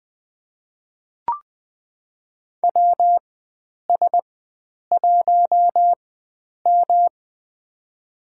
Morse code, heard as a keyed tone, about 20 wpm, spaced as for 9 wpm, 700 Hz. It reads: WS1M
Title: USA Call Signs - 4 Characters - 4x Character Spacing - 20wpm